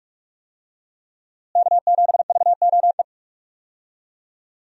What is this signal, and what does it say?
Morse code, heard as a keyed tone, about 44 wpm, 700 Hz. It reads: K7VOE
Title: USA Call Signs - 5 Characters - Repeated 2x - 45wpm